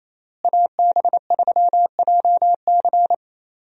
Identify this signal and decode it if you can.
Morse code, heard as a keyed tone, about 28 wpm, 700 Hz. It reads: AB3JC
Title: USA Call Signs - 5 Characters - Repeated 2x - 28wpm